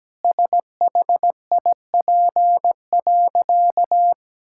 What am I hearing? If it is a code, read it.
Morse code, heard as a keyed tone, about 17 wpm, 700 Hz. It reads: SHIP.
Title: Sentences from Top 1000 Words - Review Part 5 - 17wpm